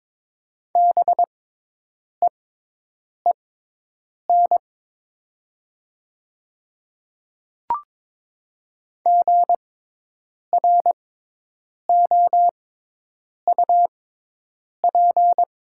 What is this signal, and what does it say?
Morse code, heard as a keyed tone, about 22 wpm, spaced as for 8 wpm, 700 Hz. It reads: BEEN GROUP
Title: 6x Character Spacing - Top 300 Words - 22wpm